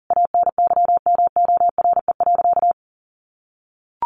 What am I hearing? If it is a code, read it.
Morse code, heard as a keyed tone, about 40 wpm, 700 Hz. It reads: ANYMORE.
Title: Additional Easy Phrases 40wpm